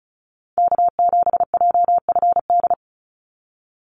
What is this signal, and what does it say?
Morse code, heard as a keyed tone, about 34 wpm, 700 Hz. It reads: K7JFD
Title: NA SOTA Chaser Callsigns 35wpm